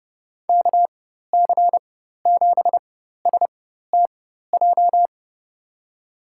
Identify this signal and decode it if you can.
Morse code, heard as a keyed tone, about 31 wpm, spaced as for 14 wpm, 700 Hz. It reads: KC7STJ
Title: USA Call Signs - 6 Characters - 4x Character Spacing - 30wpm